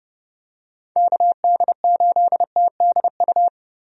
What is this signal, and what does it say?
Morse code, heard as a keyed tone, about 30 wpm, 700 Hz. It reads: KD8TDU
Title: USA Call Signs - 6 Characters - Repeated 3x - 30wpm